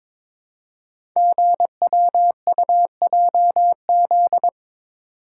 Morse code, heard as a keyed tone, about 22 wpm, 700 Hz. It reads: GWUJZ
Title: Code Groups - 5 Letters - 22wpm